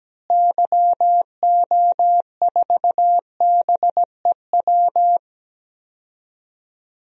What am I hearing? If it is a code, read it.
Morse code, heard as a keyed tone, about 17 wpm, 700 Hz. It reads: YO4BEW
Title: DXCC-100 Call Signs - 6 Characters - Repeated 3x - 17wpm